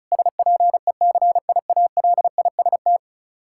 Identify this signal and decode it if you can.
Morse code, heard as a keyed tone, about 34 wpm, 700 Hz. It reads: SPECIALIST
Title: Practice QSO - Exchange 3 - More Info - 35wpm